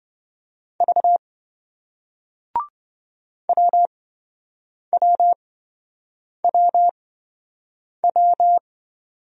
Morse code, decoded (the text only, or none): V WWWW